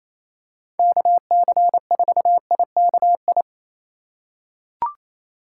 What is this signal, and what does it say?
Morse code, heard as a keyed tone, about 28 wpm, 700 Hz. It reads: KC4IKI
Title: USA Call Signs - 6 Characters - 4x Character Spacing - 28wpm